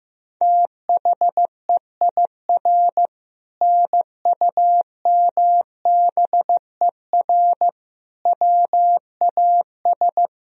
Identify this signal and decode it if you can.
Morse code, heard as a keyed tone, about 15 wpm, 700 Hz. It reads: THEIR NUMBER WAS
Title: Sentences from Top 1000 Words - Review Part 1 - 15wpm